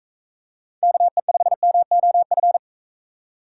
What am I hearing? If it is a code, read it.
Morse code, heard as a keyed tone, about 41 wpm, 700 Hz. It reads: KE5MOF